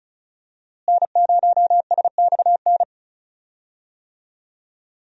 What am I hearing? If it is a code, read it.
Morse code, heard as a keyed tone, about 36 wpm, 700 Hz. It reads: N0SXN
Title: USA Call Signs - 5 Characters - Repeated 2x - 35wpm